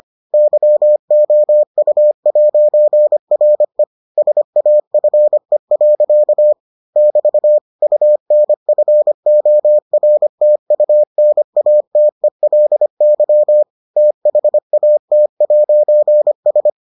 Morse code, heard as a keyed tone, about 25 wpm, 600 Hz. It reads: YOU'RE SAFE. <BT> UNFORTUNATELY THAT'S